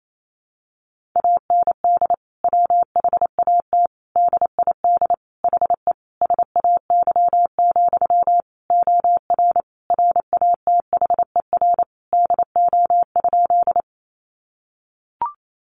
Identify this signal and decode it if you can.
Morse code, heard as a keyed tone, about 28 wpm, 700 Hz. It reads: AND WHAT DID HE SAY, OR RATHER DO?
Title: Sentences from Top 700 Words 28wpm